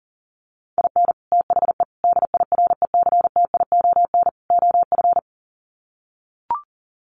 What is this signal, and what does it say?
Morse code, heard as a keyed tone, about 40 wpm, 700 Hz. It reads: IN THE DIRECTION OF
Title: Sets of 4 Words 40wpm